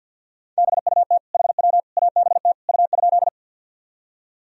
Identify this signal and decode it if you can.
Morse code, heard as a keyed tone, about 53 wpm, 700 Hz. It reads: BUT HW ABT U?
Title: Sentences from Top 100 Words and Abbreviations 50wpm